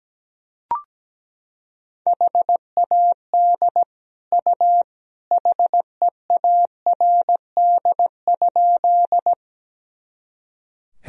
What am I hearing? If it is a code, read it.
Morse code, heard as a keyed tone, about 17 wpm, 700 Hz. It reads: HAD U HEARD?
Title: Sentences from Top 400 Words and Abbreviations 17wpm